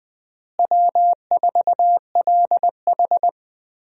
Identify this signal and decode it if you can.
Morse code, heard as a keyed tone, about 20 wpm, 700 Hz. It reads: W4LH
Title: USA Call Signs - 4 Characters - Repeated 2x - 20wpm